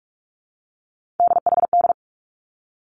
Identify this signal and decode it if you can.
Morse code, heard as a keyed tone, about 46 wpm, 700 Hz. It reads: DHD
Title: Three Challenging Letter-Number Combinations 45wpm